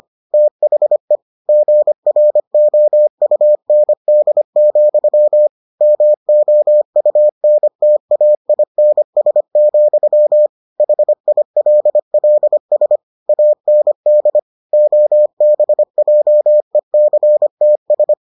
Morse code, heard as a keyed tone, about 25 wpm, 600 Hz. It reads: THE GROUND, MOUNTAINS, HILLS AND OBJECTS